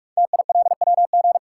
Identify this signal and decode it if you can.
Morse code, heard as a keyed tone, about 44 wpm, 700 Hz. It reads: TILWG
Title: Code Groups - 5 Letters - 45wpm